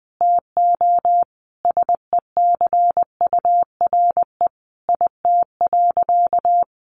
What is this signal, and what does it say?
Morse code, heard as a keyed tone, about 20 wpm, 700 Hz. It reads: TO SECURE IT.
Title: Sentences from Top 1200 Words 20wpm